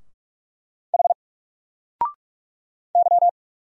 Morse code, heard as a keyed tone, about 44 wpm, 700 Hz. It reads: H Y
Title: Single Letter-Number 45wpm